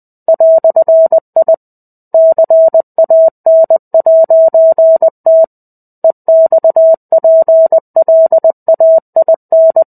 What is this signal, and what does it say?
Morse code, heard as a keyed tone, about 20 wpm, 650 Hz. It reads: "I CAN'T EXPLAIN
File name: 05-Chapter-05 20 WPM